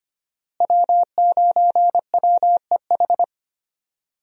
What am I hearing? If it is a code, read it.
Morse code, heard as a keyed tone, about 25 wpm, 700 Hz. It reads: W9WEH
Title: USA Call Signs - 5 Characters - Repeated 2x - 25wpm